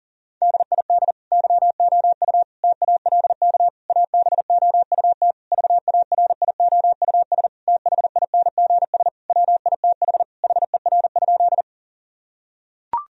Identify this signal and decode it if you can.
Morse code, heard as a keyed tone, about 37 wpm, 690 Hz. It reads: DID YOU TALK ABOUT VARIOUS THINGS WITH HER?
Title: Sentences from Top 1000 Words - Review Part 4 - 40wpm